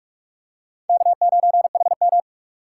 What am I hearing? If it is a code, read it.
Morse code, heard as a keyed tone, about 46 wpm, 700 Hz. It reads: K9HM
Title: USA Call Signs - 3x Character Spacing - 45wpm